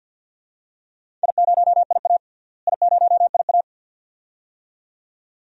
Morse code, heard as a keyed tone, about 50 wpm, 700 Hz. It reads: I0IA I0IA
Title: DXCC-100 Call Signs - 4 Characters - Repeated 2x - 50wpm